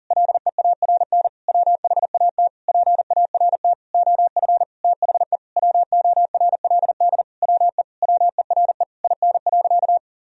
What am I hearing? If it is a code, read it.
Morse code, heard as a keyed tone, about 40 wpm, 700 Hz. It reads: LEARN WHAT PART OF THE WORLD WE WERE IN.
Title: Sentences from Top 1000 Words - Review Part 3 - 40wpm